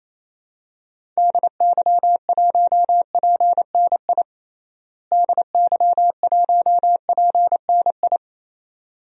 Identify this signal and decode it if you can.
Morse code, heard as a keyed tone, about 28 wpm, 700 Hz. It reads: DY1PNI DY1PNI
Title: DXCC-100 Call Signs - 6 Characters - Repeated 2x - 28wpm